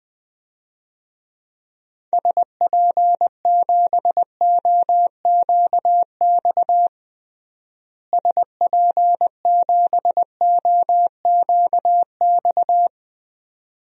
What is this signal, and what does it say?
Morse code, heard as a keyed tone, about 20 wpm, 700 Hz. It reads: SP7OQX SP7OQX
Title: DXCC-100 Call Signs - 6 Characters - Repeated 2x - 20wpm